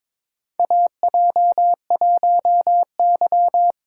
Morse code, heard as a keyed tone, about 23 wpm, 700 Hz. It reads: AJ1Y
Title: USA Call Signs - 4 Characters - 4x Character Spacing - 22wpm